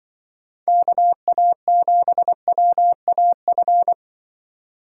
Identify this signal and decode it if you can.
Morse code, heard as a keyed tone, about 25 wpm, 700 Hz. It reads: KA7WAF
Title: Speed Racing - USA 2x3 Call Signs 20wpm